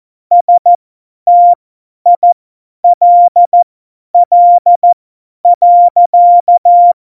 Morse code, heard as a keyed tone, about 14 wpm, spaced as for 10 wpm, 700 Hz. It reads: STILL.
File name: Morsekurs / Witze / Witz 05. Januar 2021, 50 F70